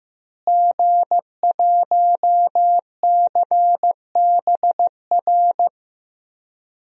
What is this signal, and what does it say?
Morse code, heard as a keyed tone, about 15 wpm, 700 Hz. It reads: G1CBR
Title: DXCC-100 Call Signs - 5 Characters - Repeated 3x - 15wpm